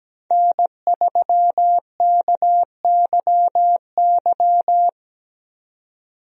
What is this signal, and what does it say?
Morse code, heard as a keyed tone, about 17 wpm, 700 Hz. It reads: N3KYY
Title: USA Call Signs - 5 Characters - 17wpm